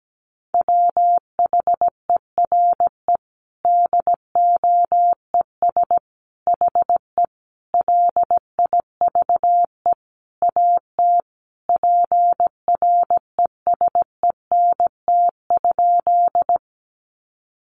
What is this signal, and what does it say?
Morse code, heard as a keyed tone, about 17 wpm, 700 Hz. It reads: WHERE DOES HE LIVE AT PRESENT?
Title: Sentences from Top 800 Words 17wpm